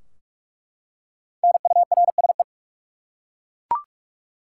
Morse code, heard as a keyed tone, about 47 wpm, 700 Hz. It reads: NURSE
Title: Five Letter Words - 45wpm